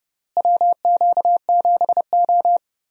Morse code, heard as a keyed tone, about 30 wpm, 700 Hz. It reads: WQ7O
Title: USA Call Signs - 4 Characters - Repeated 2x - 30wpm